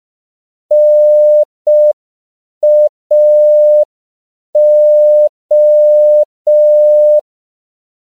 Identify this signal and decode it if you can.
Morse code, heard as a keyed tone, about 5 wpm, 600 Hz. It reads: NAO
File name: CW / 5PPM